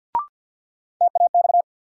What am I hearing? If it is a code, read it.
Morse code, heard as a keyed tone, about 52 wpm, 700 Hz. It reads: TAX
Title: Sentences from Top 1000 Words - Review Part 5 - New Word Drill - 50wpm